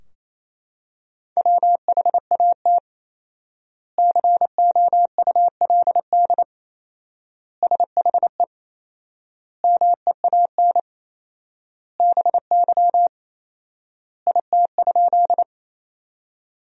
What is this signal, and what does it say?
Morse code, heard as a keyed tone, about 28 wpm, 700 Hz. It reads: WHAT COULD SHE MEAN BY IT?
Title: 4x Word-Spacing - Sentences from Top 200 Words - 28wpm